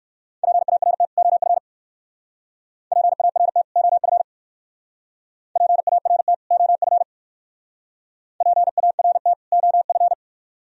Morse code, heard as a keyed tone, about 53 wpm, 700 Hz. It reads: PART OF PART OF PART OF PART OF
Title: Speed-Racing - Sets of 2 Words 45wpm